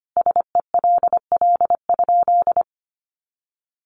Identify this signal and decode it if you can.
Morse code, heard as a keyed tone, about 25 wpm, 700 Hz. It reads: SELL?
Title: Sentences from Top 1100 Words 25wpm